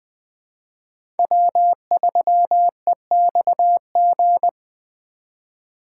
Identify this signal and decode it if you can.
Morse code, heard as a keyed tone, about 20 wpm, 700 Hz. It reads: W3EXG
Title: Speed Racing - USA 1x3 Call Signs 17wpm